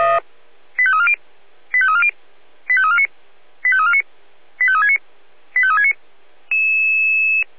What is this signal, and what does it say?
Five-tone selective call (ZVEI), heard as quick runs of five tones, tones of about 70 ms. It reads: 85340 85340 85349 85349 85369 85369
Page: SDR/ZVEI